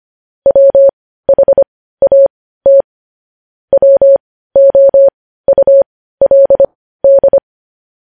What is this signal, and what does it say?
Morse code, heard as a keyed tone, about 25 wpm, spaced as for 15 wpm, 550 Hz. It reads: WHAT WOULD
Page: news-headlines-in-morse-code-at-15-wpm